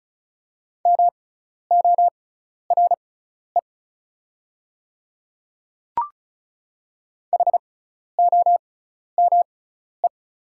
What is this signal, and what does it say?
Morse code, heard as a keyed tone, about 36 wpm, spaced as for 12 wpm, 690 Hz. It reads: MORE HOME
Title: 6x Character Spacing - Top 300 Words - 35wpm